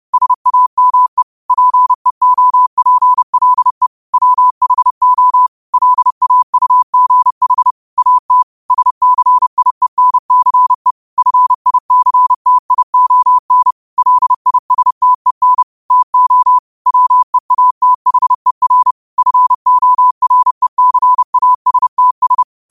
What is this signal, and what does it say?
Morse code, heard as a keyed tone, about 30 wpm, 1000 Hz. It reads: SAME PEOPLE WHO LAUGH AT SCIENCE FICTION LISTEN TO WEATHER FORECASTS
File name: qotdfiles / 30wpm